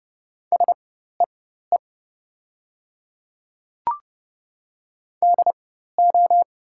Morse code, heard as a keyed tone, about 30 wpm, spaced as for 14 wpm, 700 Hz. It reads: SEE DO
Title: 4x Character Spacing - Top 300 Words - 30wpm